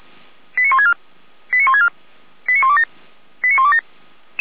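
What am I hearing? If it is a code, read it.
Five-tone selective call (ZVEI), heard as quick runs of five tones, tones of about 70 ms. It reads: 89164 89164 89127 89127
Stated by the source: ZVEI five-tone selective call 89354, received on SDR.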